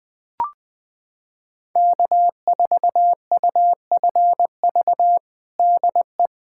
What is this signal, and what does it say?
Morse code, heard as a keyed tone, about 21 wpm, 700 Hz. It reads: K4UFV DE
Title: USA Callsign DE Callsign - 5 Characters - 20wpm